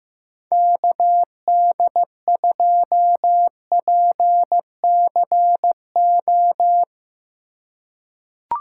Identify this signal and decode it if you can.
Morse code, heard as a keyed tone, about 15 wpm, 700 Hz. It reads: KD2PCO
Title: USA Call Signs - 6 Characters - 2x Character Spacing - 15wpm